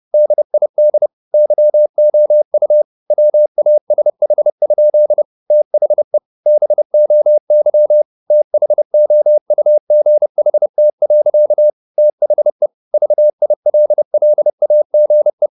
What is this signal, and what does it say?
Morse code, heard as a keyed tone, about 30 wpm, 600 Hz. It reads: DID YOU WASH? THE BOY THOUGHT. THE VILLAGE
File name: the old man and the sea / the old man and the sea 1 30wpm